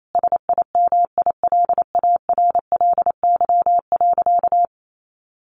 Morse code, heard as a keyed tone, about 28 wpm, 700 Hz. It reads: SIMILARLY.
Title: Sentences from Top 1400 Words 28wpm